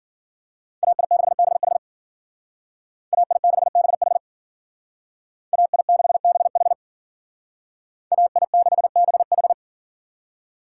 Morse code, heard as a keyed tone, about 48 wpm, 700 Hz. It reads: AI6BH AI6BH AI6BH AI6BH
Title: Speed Racing - USA 2x2 Call Signs 40wpm